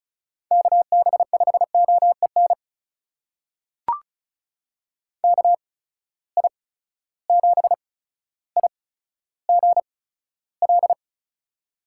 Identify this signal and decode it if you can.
Morse code, heard as a keyed tone, about 36 wpm, spaced as for 15 wpm, 700 Hz. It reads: KB5OEN KI7IGL
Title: USA Call Signs - 6 Characters - 8x Character Spacing - 35wpm